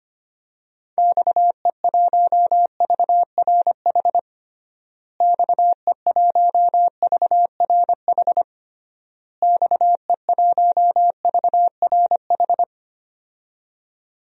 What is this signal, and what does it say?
Morse code, heard as a keyed tone, about 25 wpm, 700 Hz. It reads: XE1VRH XE1VRH XE1VRH
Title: DXCC-100 Call Signs - 6 Characters - Repeated 3x - 25wpm